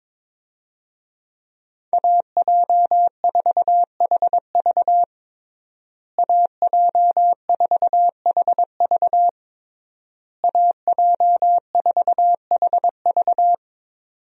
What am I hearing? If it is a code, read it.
Morse code, heard as a keyed tone, about 21 wpm, 700 Hz. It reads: AJ4HV AJ4HV AJ4HV
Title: USA Call Signs - 5 Characters - Repeated 3x - 22wpm